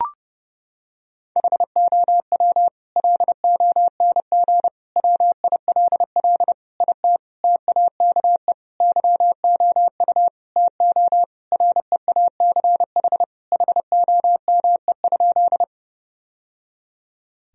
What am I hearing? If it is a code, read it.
Morse code, heard as a keyed tone, about 30 wpm, 700 Hz. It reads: HOW LONG WILL IT TAKE YOU TO REACH HOME?
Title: Sentences from Top 1000 Words - Review Part 2 - 30wpm